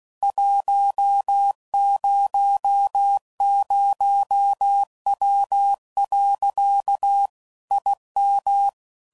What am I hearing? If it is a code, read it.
Morse code, heard as a keyed tone, about 16 wpm, 800 Hz. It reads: 100W. IM
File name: CW Code Groups @15WPM / QSOs / WD0FBT Spaced